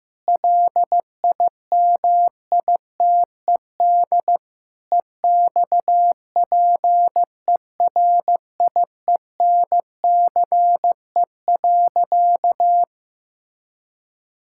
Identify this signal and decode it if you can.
Morse code, heard as a keyed tone, about 15 wpm, 700 Hz. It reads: LIMITED EXPERIENCE.
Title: Sentences from Top 1000 Words - Review Part 6 - 15wpm